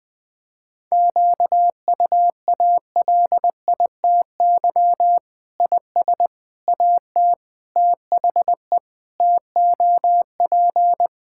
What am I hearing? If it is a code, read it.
Morse code, heard as a keyed tone, about 19 wpm, 700 Hz. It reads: QUALITY IS AT THE TOP
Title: Sentences with 6-Word Phrases - 20wpm